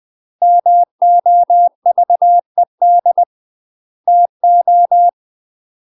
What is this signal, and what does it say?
Morse code, heard as a keyed tone, about 20 wpm, 700 Hz. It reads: MOVED TO
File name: UK MASTER QSOs TRAINING / 20WPM / UK MASTER QSOs 0073